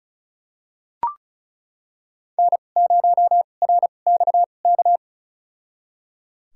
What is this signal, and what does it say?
Morse code, heard as a keyed tone, about 36 wpm, spaced as for 25 wpm, 700 Hz. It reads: N0RXK
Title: USA Call Signs - 5 Characters - 2x Character Spacing - 35wpm